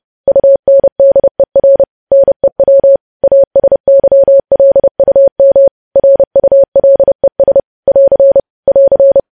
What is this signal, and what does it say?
Morse code, heard as a keyed tone, about 30 wpm, 550 Hz. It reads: UNDER NEW ASYLUM RULES <AR> <AR>
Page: news-headlines-in-morse-code-at-30-wpm